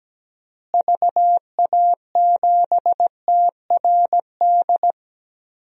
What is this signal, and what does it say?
Morse code, heard as a keyed tone, about 17 wpm, 700 Hz. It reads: VA7TRD
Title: DXCC-100 Call Signs - 6 Characters - Repeated 2x - 17wpm